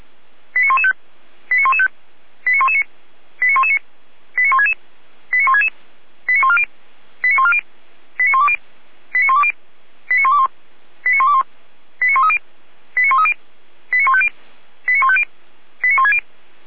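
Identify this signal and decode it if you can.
Five-tone selective call (ZVEI), heard as quick runs of five tones, tones of about 80 ms. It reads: 89115 89115 89119 89119 89166 89166 89140 89140 89120 89120 89121 89121 89130 89130 89150 89150 89160
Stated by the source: ZVEI five-tone selective call 89078 received by SDR.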